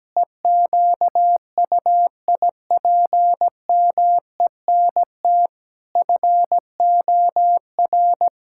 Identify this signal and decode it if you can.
Morse code, heard as a keyed tone, about 17 wpm, 700 Hz. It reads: EQUIPMENT FOR
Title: Sentences from Top 1000 Words - Review Part 5 - 17wpm